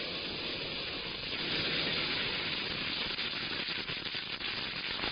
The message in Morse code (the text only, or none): none